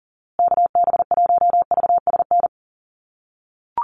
Morse code, heard as a keyed tone, about 40 wpm, 700 Hz. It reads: KB1VSN
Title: USA Call Signs Repeated 2x 40wpm